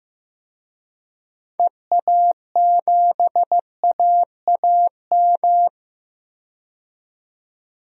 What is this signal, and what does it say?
Morse code, heard as a keyed tone, about 15 wpm, 700 Hz. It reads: EA7AAM
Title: DXCC-100 Call Signs - 6 Characters - 15wpm